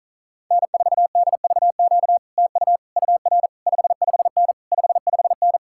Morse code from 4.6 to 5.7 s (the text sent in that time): 55N